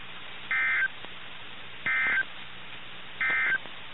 Emergency Alert System (SAME) tones, heard as harsh buzzing bursts. It is heard from 0.4 to 3.7 s.
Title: NOAA Weather Radio - KIH61